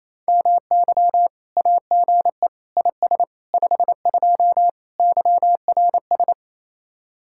Morse code, heard as a keyed tone, about 37 wpm, 700 Hz. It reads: MY AGE IS 52 YRS